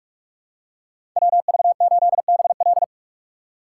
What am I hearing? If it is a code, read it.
Morse code, heard as a keyed tone, about 42 wpm, 700 Hz. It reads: WV8BL
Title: USA Call Signs - 5 Characters - 3x Character Spacing - 45wpm